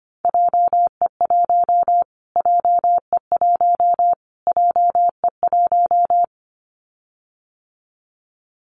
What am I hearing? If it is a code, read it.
Morse code, heard as a keyed tone, about 25 wpm, 700 Hz. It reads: JE1 JE1 JE1
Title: Callsign Prefixes for Top 25 DXCC - Repeated 3x - 25wpm